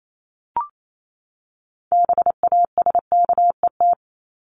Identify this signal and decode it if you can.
Morse code, heard as a keyed tone, about 28 wpm, 700 Hz. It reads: BASKET
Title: Top 1751-2000 Words 28wpm